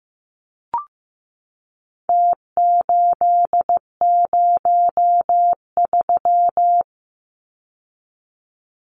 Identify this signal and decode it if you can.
Morse code, heard as a keyed tone, about 15 wpm, 700 Hz. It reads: T803
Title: Numbers - Contest Style - 4-Digit 15wpm